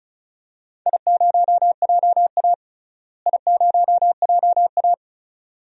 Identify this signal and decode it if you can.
Morse code, heard as a keyed tone, about 36 wpm, 700 Hz. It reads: I0JA I0JA
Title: DXCC-100 Call Signs - 4 Characters - Repeated 2x - 35wpm